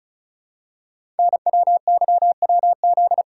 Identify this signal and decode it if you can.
Morse code, heard as a keyed tone, about 36 wpm, 700 Hz. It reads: NWYWZ
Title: Code Groups - 5 Letters - 35wpm